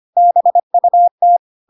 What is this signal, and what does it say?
Morse code, heard as a keyed tone, about 25 wpm, 700 Hz. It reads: BUT
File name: Pinocchio / Pinocchio ch6 25 25 ik1hge